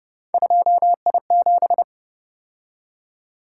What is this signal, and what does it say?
Morse code, heard as a keyed tone, about 29 wpm, 700 Hz. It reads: 2I7
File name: Speed-Racing - Three Letter-Number Combinations - 25wpm